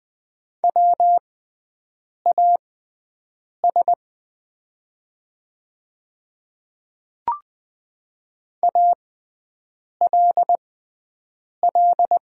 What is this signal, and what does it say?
Morse code, heard as a keyed tone, about 20 wpm, spaced as for 7 wpm, 700 Hz. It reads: WAS ALL
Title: 6x Character Spacing - Top 100 Words - 20wpm